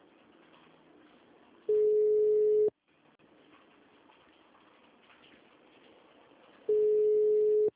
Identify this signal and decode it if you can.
Ringback tone (Europe, 425 Hz).
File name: out- -2006- - - .89748